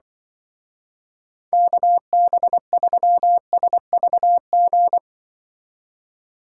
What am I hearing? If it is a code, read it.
Morse code, heard as a keyed tone, about 23 wpm, 700 Hz. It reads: KB3SVG
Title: Speed Racing - USA 2x3 Call Signs 20wpm